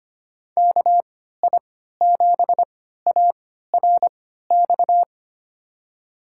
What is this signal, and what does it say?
Morse code, heard as a keyed tone, about 25 wpm, spaced as for 14 wpm, 690 Hz. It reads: KI7ARX
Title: USA Call Signs - 6 Characters - 3x Character Spacing - 25wpm